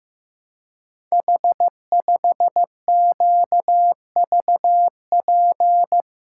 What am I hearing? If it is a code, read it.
Morse code, heard as a keyed tone, about 15 wpm, 700 Hz. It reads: H5QVP
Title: Code Groups - 5 Letter:Numbers - 15wpm